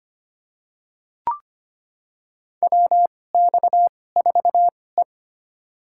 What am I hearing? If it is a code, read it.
Morse code, heard as a keyed tone, about 25 wpm, spaced as for 18 wpm, 700 Hz. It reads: WX4E
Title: USA Call Signs - 4 Characters - 2x Character Spacing - 25wpm